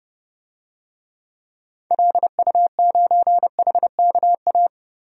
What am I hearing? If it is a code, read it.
Morse code, heard as a keyed tone, about 30 wpm, 700 Hz. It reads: LU9HKA
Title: DXCC-100 Call Signs - 6 Characters - 30wpm